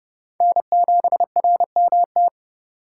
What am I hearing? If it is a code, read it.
Morse code, heard as a keyed tone, about 30 wpm, 700 Hz. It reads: N7RMT
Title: USA Call Signs - 8x Character Spacing - 30wpm